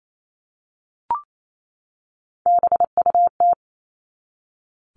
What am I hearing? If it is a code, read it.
Morse code, heard as a keyed tone, about 28 wpm, 700 Hz. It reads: BUT
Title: Top 300 Words 28wpm